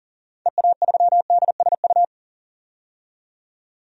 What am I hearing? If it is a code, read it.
Morse code, heard as a keyed tone, about 40 wpm, 700 Hz. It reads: EA3DSU
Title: DXCC-100 Call Signs - 6 Characters - 40wpm